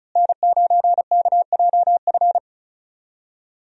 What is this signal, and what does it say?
Morse code, heard as a keyed tone, about 36 wpm, 700 Hz. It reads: N9KJF
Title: USA Call Signs - 5 Characters - Repeated 3x - 35wpm